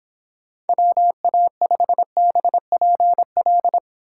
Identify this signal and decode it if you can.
Morse code, heard as a keyed tone, about 26 wpm, 700 Hz. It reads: WA5BPL